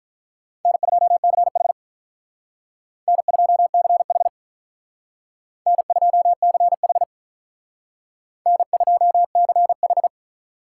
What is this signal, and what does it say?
Morse code, heard as a keyed tone, about 44 wpm, 700 Hz. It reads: N2CH N2CH N2CH N2CH